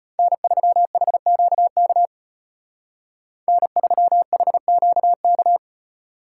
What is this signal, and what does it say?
Morse code, heard as a keyed tone, about 32 wpm, 700 Hz. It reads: N3HQK N3HQK